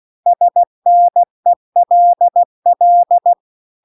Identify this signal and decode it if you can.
Morse code, heard as a keyed tone, about 16 wpm, 700 Hz. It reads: SNELL